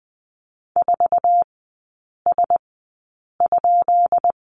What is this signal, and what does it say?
Morse code, heard as a keyed tone, about 20 wpm, spaced as for 8 wpm, 700 Hz. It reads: 4S?
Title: 178 Lesson 29e - Slash - Three Characters 20wpm